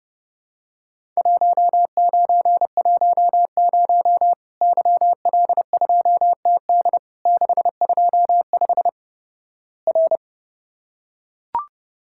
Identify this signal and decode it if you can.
Morse code, heard as a keyed tone, about 30 wpm, 700 Hz. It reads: 1910 YL2TD 625 R
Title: Worked All Europe DX Contest - QTC Exchange - 30wpm